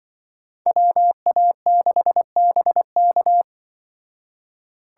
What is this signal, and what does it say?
Morse code, heard as a keyed tone, about 24 wpm, 700 Hz. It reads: WA6BK
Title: Speed Racing - USA 2x2 Call Signs 20wpm